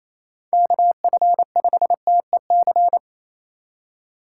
Morse code, heard as a keyed tone, about 27 wpm, 700 Hz. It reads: KF5TEC
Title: USA Call Signs - 6 Characters - 6x Character Spacing - 28wpm